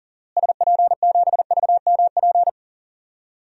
Morse code, heard as a keyed tone, about 40 wpm, 700 Hz. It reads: SP7VMP
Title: DXCC-100 Call Signs - 6 Characters - Repeated 3x - 40wpm